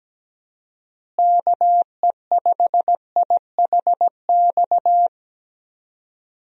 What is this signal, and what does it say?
Morse code, heard as a keyed tone, about 17 wpm, 700 Hz. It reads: KE5IHX
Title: USA Call Signs - 6 Characters - Repeated 3x - 17wpm